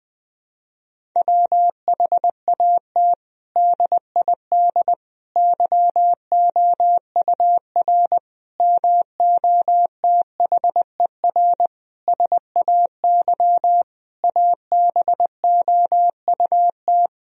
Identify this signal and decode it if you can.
Morse code, heard as a keyed tone, about 20 wpm, 700 Hz. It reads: WHAT DID YOUR MOTHER SAY ABOUT
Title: Sentences from Top 1000 Words - Review Part 3 - 20wpm